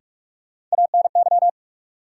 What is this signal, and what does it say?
Morse code, heard as a keyed tone, about 46 wpm, 700 Hz. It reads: ANY
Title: Sentences from Top 200 Words - New Word Drill - 45wpm